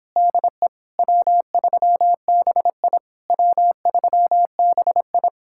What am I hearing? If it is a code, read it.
Morse code, heard as a keyed tone, about 29 wpm, 700 Hz. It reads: DE W3BI W3BI